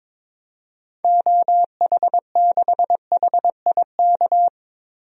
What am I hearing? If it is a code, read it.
Morse code, heard as a keyed tone, about 22 wpm, 700 Hz. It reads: OH6HIK
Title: DXCC-100 Call Signs - 6 Characters - 22wpm